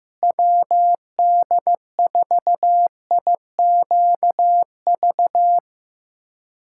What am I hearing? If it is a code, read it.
Morse code, heard as a keyed tone, about 15 wpm, 700 Hz. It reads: WD4IQV
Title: USA Callsign DE Callsign - 6 Characters - 15wpm